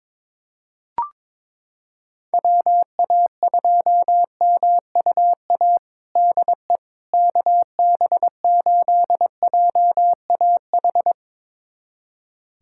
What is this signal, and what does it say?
Morse code, heard as a keyed tone, about 22 wpm, 700 Hz. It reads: WA2MUA DE KB8JAH
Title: USA Callsign DE Callsign 22wpm